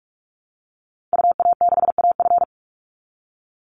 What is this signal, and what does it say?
Morse code, heard as a keyed tone, about 44 wpm, 700 Hz. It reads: UA6AF
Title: Call Signs Repeated 2x - 45wpm